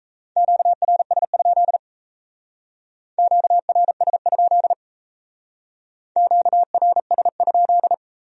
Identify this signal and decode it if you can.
Morse code, heard as a keyed tone, about 36 wpm, 700 Hz. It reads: QRS? QRS? QRS?